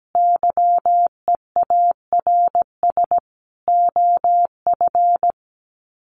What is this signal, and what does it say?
Morse code, heard as a keyed tone, about 17 wpm, 700 Hz. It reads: YEARS OF
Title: Sets of 5 Words Encore 17wpm